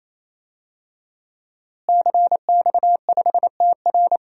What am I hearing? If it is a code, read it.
Morse code, heard as a keyed tone, about 28 wpm, 700 Hz. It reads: CX5TR
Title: DXCC-100 Call Signs - 5 Characters - Repeated 3x - 28wpm